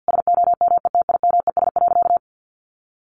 Morse code, heard as a keyed tone, about 50 wpm, 700 Hz. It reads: SOMETIMES.
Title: Sentences from Top 600 Words 50wpm